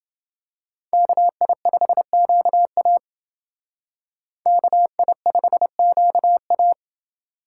Morse code, heard as a keyed tone, about 29 wpm, 700 Hz. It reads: KI5QA KI5QA